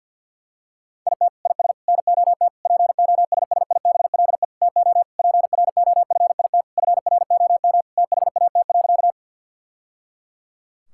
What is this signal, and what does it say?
Morse code, heard as a keyed tone, about 50 wpm, 700 Hz. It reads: IT IS NOT POSSIBLE TO PROFIT FROM THAT.
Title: Sentences from Top 1000 Words - Review Part 5 - 50wpm